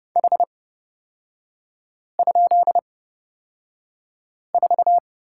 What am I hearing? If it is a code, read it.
Morse code, heard as a keyed tone, about 30 wpm, spaced as for 5 wpm, 700 Hz. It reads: H?4
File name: Lesson 37e - Letter x - Three Characters 30wpm